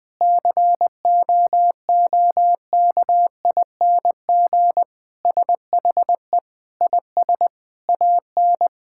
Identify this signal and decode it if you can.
Morse code, heard as a keyed tone, about 21 wpm, 700 Hz. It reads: COOKING SHE IS AN
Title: Sentences with 5-Word Phrases - 20wpm